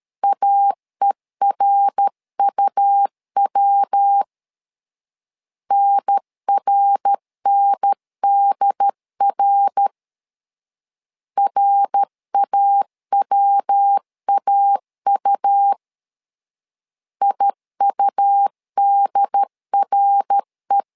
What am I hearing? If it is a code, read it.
Morse code, heard as a keyed tone, about 13 wpm, 800 Hz. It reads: AERUW NRNDR RAWAU IUDRE
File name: LEZIONI / 3 MWRO / 3 MWR 12wpm 1711